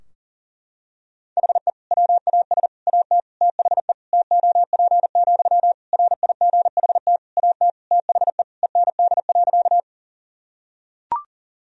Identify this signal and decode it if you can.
Morse code, heard as a keyed tone, about 40 wpm, 700 Hz. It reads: HE WAS AT THE TOP, RIGHT AT THE END.
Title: Sentences from Top 1000 Words - Review Part 2 - 40wpm